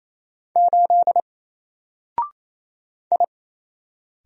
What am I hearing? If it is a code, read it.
Morse code, heard as a keyed tone, about 27 wpm, 700 Hz. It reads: 8 I
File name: Single-Letter-Number-28wpm